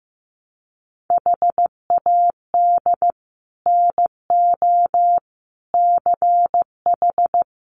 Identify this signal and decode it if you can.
Morse code, heard as a keyed tone, about 15 wpm, 700 Hz. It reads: HAD NO CH
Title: Sets of 5 Words 15wpm